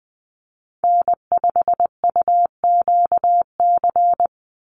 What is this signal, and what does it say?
Morse code, heard as a keyed tone, about 20 wpm, 700 Hz. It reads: N5UQC
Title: USA Call Signs 20wpm